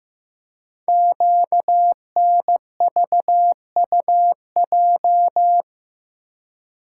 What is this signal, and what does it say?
Morse code, heard as a keyed tone, about 15 wpm, 700 Hz. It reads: QNVUJ
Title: Code Groups - 5 Letters - 15wpm